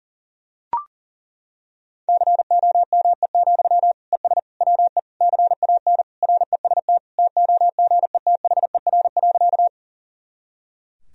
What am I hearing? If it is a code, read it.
Morse code, heard as a keyed tone, about 40 wpm, 700 Hz. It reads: COME, ES WE CAN REST TOGETHER.
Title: Sentences from Top 500 Words and Abbreviations 40wpm